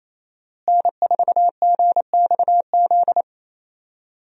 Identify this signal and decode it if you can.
Morse code, heard as a keyed tone, about 27 wpm, 700 Hz. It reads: N4GXZ
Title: USA Call Signs - 5 Characters - Repeated 3x - 28wpm